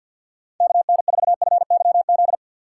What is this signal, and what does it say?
Morse code, heard as a keyed tone, about 53 wpm, 700 Hz. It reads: KN4FYZ
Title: USA Call Signs - 4x Character Spacing - 50wpm